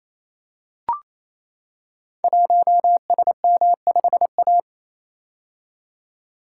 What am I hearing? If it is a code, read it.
Morse code, heard as a keyed tone, about 27 wpm, 700 Hz. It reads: 1SM5A
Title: Code Groups - 5 Letter:Numbers - 28wpm